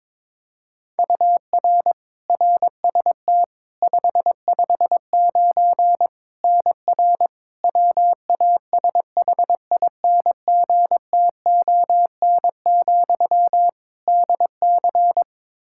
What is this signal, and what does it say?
Morse code, heard as a keyed tone, about 22 wpm, 700 Hz. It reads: UR RST 559 NR WASHINGTON, DC
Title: K1USN SST QSO Practice 22wpm